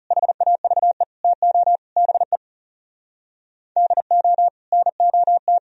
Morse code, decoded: HAVE TO BE DO NOT

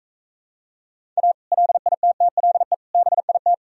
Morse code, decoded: A LITTLE BIT

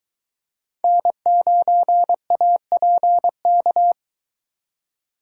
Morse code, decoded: N9APK

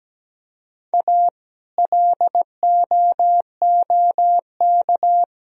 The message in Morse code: A LOOK

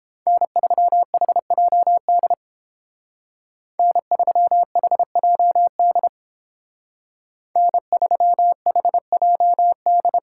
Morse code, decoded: N3HJD N3HJD N3HJD

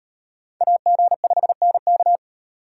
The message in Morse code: AG5NK

